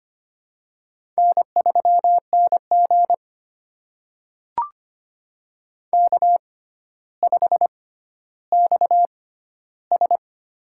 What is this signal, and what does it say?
Morse code, heard as a keyed tone, about 25 wpm, spaced as for 13 wpm, 700 Hz. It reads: N3NG K5XS
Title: USA Call Signs - 4 Characters - 6x Character Spacing - 25wpm